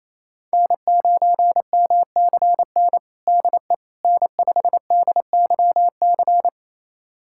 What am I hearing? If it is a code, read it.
Morse code, heard as a keyed tone, about 27 wpm, 700 Hz. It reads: N9MCN DE N5DYC
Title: USA Callsign DE Callsign - 5 Characters - 28wpm